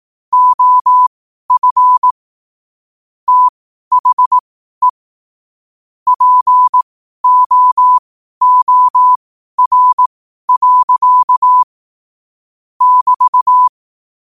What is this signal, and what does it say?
Morse code, heard as a keyed tone, about 18 wpm, spaced as for 12 wpm, 1000 Hz. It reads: OF THE POOR. <BT>
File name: qotdfiles / 12wpm